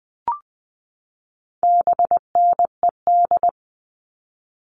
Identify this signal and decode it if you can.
Morse code, heard as a keyed tone, about 20 wpm, 700 Hz. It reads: BNED
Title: All Cut Numbers - 4-Digit 20wpm